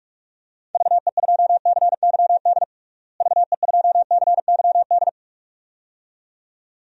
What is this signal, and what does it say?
Morse code, heard as a keyed tone, about 46 wpm, 700 Hz. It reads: VE2CYD VE2CYD
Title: DXCC-100 Call Signs - 6 Characters - Repeated 2x - 45wpm